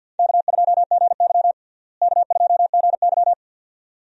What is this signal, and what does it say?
Morse code, heard as a keyed tone, about 47 wpm, 700 Hz. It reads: K2GY K2GY
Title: USA Call Signs - 4 Characters - Repeated 2x - 50wpm